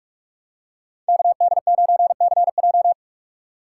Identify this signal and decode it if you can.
Morse code, heard as a keyed tone, about 46 wpm, 700 Hz. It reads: KD9CJ